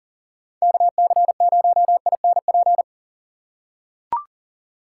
Morse code, heard as a keyed tone, about 37 wpm, 700 Hz. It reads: KC0INP
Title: K1USN SST QSO Practice - 40wpm